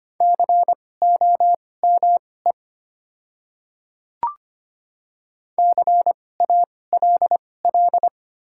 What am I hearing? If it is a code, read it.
Morse code, heard as a keyed tone, about 25 wpm, spaced as for 18 wpm, 700 Hz. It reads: COME CALL
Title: 2x Character Spacing - Top 100 Words - 25wpm